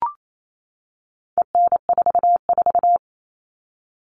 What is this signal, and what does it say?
Morse code, heard as a keyed tone, about 28 wpm, 700 Hz. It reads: EN44
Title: All Cut Numbers - 4-Digit 28wpm